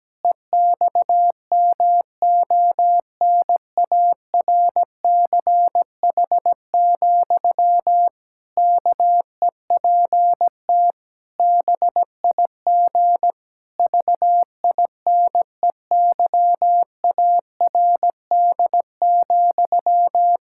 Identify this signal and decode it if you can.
Morse code, heard as a keyed tone, about 17 wpm, 700 Hz. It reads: EXMONARCH, KEPT BIG VINEYARD,
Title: Pangram Challenge v2 - 17wpm